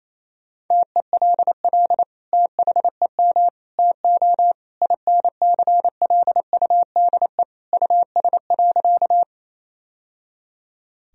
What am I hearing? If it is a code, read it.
Morse code, heard as a keyed tone, about 28 wpm, 700 Hz. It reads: TELL THEM TO INCLUDE US.
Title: Sentences from Top 1000 Words - Review Part 3 - 28wpm